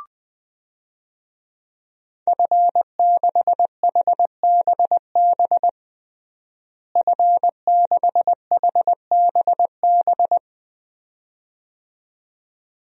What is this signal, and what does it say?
Morse code, heard as a keyed tone, about 20 wpm, 700 Hz. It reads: F6HBB F6HBB
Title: DXCC-100 Call Signs - 5 Characters - Repeated 2x - 20wpm